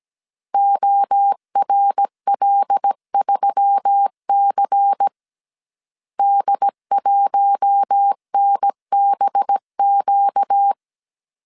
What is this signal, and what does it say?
Morse code, heard as a keyed tone, about 17 wpm, 800 Hz. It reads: ORL3C D1NBQ
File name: 16wpm